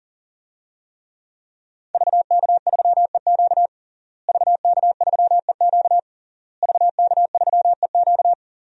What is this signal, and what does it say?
Morse code, heard as a keyed tone, about 40 wpm, 700 Hz. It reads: VK3EQ VK3EQ VK3EQ
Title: DXCC-100 Call Signs - 5 Characters - Repeated 3x - 40wpm